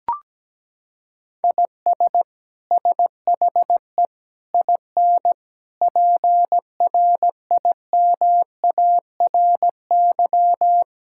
Morse code, heard as a keyed tone, about 17 wpm, 700 Hz. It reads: IS SHE IN PRIMARY
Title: Sentences from Top 1000 Words - Review Part 5 - 17wpm